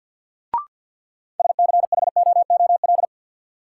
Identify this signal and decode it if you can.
Morse code, heard as a keyed tone, about 53 wpm, 700 Hz. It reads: SCHOOL
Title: Sentences from Top 300 Words - New Word Drill - 50wpm